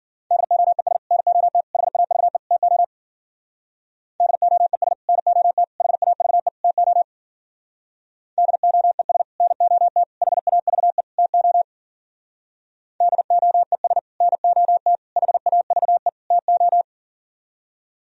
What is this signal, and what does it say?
Morse code, heard as a keyed tone, about 52 wpm, 700 Hz. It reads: DOES NOT HAVE TO DOES NOT HAVE TO DOES NOT HAVE TO DOES NOT HAVE TO